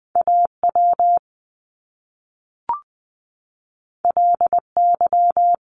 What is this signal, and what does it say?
Morse code, heard as a keyed tone, about 20 wpm, 700 Hz. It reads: A W L Y
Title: Top Two Letter Combinations in Words 20wpm